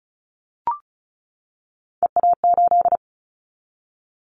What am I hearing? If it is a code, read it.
Morse code, heard as a keyed tone, about 34 wpm, 700 Hz. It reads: EA8
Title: Callsign Prefixes for Top 100 DXCC - 35wpm